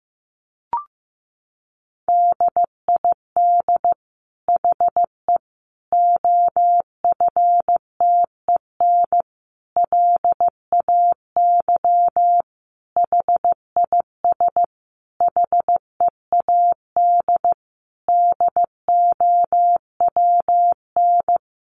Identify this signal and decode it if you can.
Morse code, heard as a keyed tone, about 15 wpm, 700 Hz. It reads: DID HE OFTEN LAY HIS HEAD DOWN
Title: Sentences from Top 1000 Words 15wpm